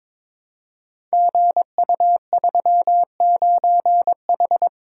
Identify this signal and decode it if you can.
Morse code, heard as a keyed tone, about 22 wpm, 700 Hz. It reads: GU39H
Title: Code Groups - 5 Letter:Numbers - 22wpm